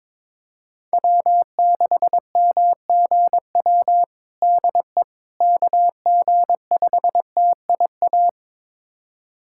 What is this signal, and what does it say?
Morse code, heard as a keyed tone, about 22 wpm, 700 Hz. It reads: W6MGW DE KG5TIA
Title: USA Callsign DE Callsign 22wpm